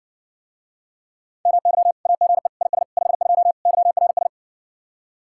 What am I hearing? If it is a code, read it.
Morse code, heard as a keyed tone, about 61 wpm, 700 Hz. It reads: MY AGE IS 52 YRS